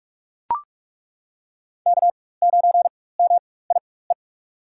Morse code, heard as a keyed tone, about 43 wpm, spaced as for 21 wpm, 700 Hz. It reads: K9MIE
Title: USA Call Signs - 5 Characters - 4x Character Spacing - 45wpm